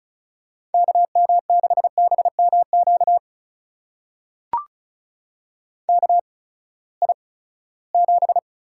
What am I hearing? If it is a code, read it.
Morse code, heard as a keyed tone, about 36 wpm, spaced as for 20 wpm, 700 Hz. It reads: KM6BMQ KI7
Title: USA Call Signs - 6 Characters - 8x Character Spacing - 35wpm